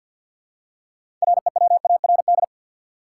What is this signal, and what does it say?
Morse code, heard as a keyed tone, about 52 wpm, 700 Hz. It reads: REWARD
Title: Six Letter Words - 50wpm